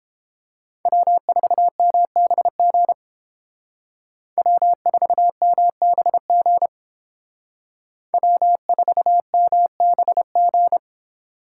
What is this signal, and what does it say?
Morse code, heard as a keyed tone, about 29 wpm, 700 Hz. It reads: W4MBG W4MBG W4MBG